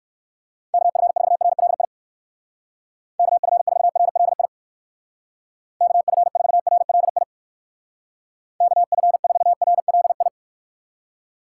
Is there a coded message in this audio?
Yes: Morse code from 0.7 to 10.3 s, heard as a keyed tone, about 59 wpm, 700 Hz. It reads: KF4RLI KF4RLI KF4RLI KF4RLI